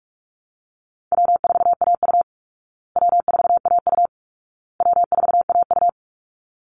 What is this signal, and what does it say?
Morse code, heard as a keyed tone, about 44 wpm, 700 Hz. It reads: W4AU W4AU W4AU
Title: Easy Call Signs Repeated 3x - 45wpm